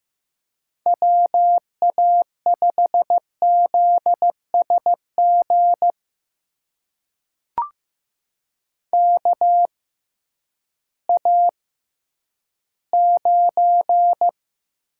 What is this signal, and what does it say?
Morse code, heard as a keyed tone, about 15 wpm, spaced as for 10 wpm, 700 Hz. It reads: WA5ZSG KA9
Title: USA Call Signs - 6 Characters - 6x Character Spacing - 15wpm